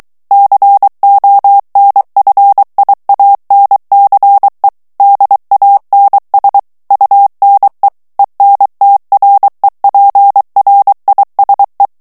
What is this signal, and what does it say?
Morse code, heard as a keyed tone, about 23 wpm, 800 Hz. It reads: CONFIANCE DANS UNE ENTREPRISE
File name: CW / FICHIERS-MP3 / FRANCAIS TEXTE CLAIR 2 VITESSE 22 MOTS MINUTES